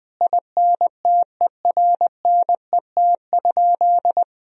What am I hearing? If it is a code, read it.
Morse code, heard as a keyed tone, about 21 wpm, 700 Hz. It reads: INTERNET?
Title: Sentences from Top 1000 Words - Review Part 4 - 20wpm